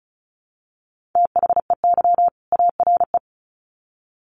Morse code, heard as a keyed tone, about 34 wpm, 700 Hz. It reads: THEY ARE
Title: Sets of 2 Words 35wpm